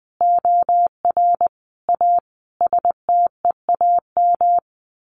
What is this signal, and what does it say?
Morse code, heard as a keyed tone, about 20 wpm, 700 Hz. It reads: OR A STEAM
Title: A Sound Of Thunder 20wpm